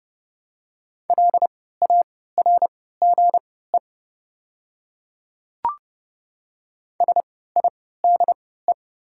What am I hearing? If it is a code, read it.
Morse code, heard as a keyed tone, about 30 wpm, spaced as for 17 wpm, 700 Hz. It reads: LARGE SIDE
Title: 3x Character Spacing - Top 200 Words - 30wpm